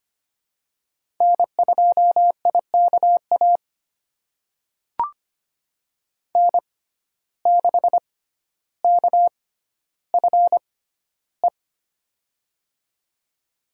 Morse code, heard as a keyed tone, about 25 wpm, spaced as for 13 wpm, 700 Hz. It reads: N2IKA N6KFE
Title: USA Call Signs - 5 Characters - 6x Character Spacing - 25wpm